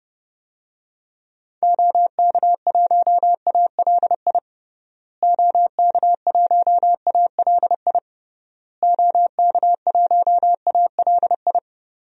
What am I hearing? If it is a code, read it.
Morse code, heard as a keyed tone, about 29 wpm, 700 Hz. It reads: OK1ALI OK1ALI OK1ALI
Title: DXCC-100 Call Signs - 6 Characters - Repeated 3x - 30wpm